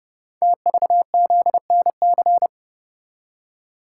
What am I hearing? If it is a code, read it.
Morse code, heard as a keyed tone, about 30 wpm, 700 Hz. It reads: TVZNC
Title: Code Groups - 5 Letter:Numbers - 30wpm